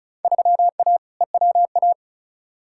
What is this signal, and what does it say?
Morse code, heard as a keyed tone, about 34 wpm, 700 Hz. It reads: 3A EWA
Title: Field Day Exchanges - 35wpm